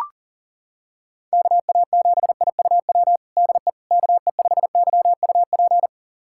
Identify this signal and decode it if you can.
Morse code, heard as a keyed tone, about 40 wpm, 700 Hz. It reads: KA7IUW DE KE5YUP
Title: USA Callsign DE Callsign - 6 Characters - 40wpm